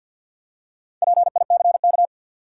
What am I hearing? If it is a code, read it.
Morse code, heard as a keyed tone, about 53 wpm, 700 Hz. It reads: PICK
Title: Four Letter Words - 50wpm